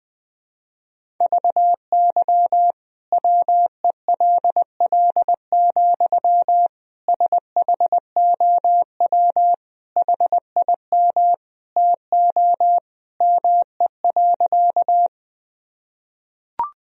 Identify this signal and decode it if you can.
Morse code, heard as a keyed tone, about 20 wpm, 700 Hz. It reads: VY WELL, SHOW HIM TO ME.
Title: Sentences from Top 200 Words and Abbreviations 20wpm